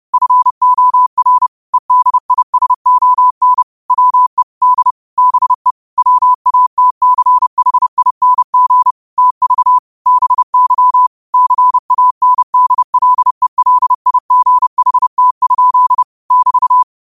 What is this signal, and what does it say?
Morse code, heard as a keyed tone, about 30 wpm, 1000 Hz. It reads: FOR EDISON WE D BE WATCHING TV BY CANDLELIGHT? <BT>
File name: qotdfiles / 30wpm